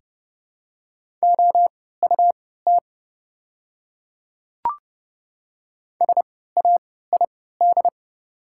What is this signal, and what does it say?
Morse code, heard as a keyed tone, about 30 wpm, spaced as for 17 wpm, 690 Hz. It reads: OUT SAID
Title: 3x Character Spacing - Top 100 Words - 30wpm